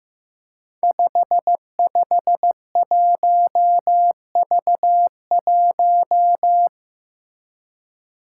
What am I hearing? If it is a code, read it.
Morse code, heard as a keyed tone, about 15 wpm, 700 Hz. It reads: 551V1
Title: Five Challenging Letter-Number Combinations 15wpm